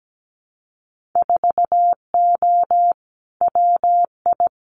Morse code, heard as a keyed tone, about 17 wpm, 700 Hz. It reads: 4O WI
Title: Winter Field Day Exchanges 17wpm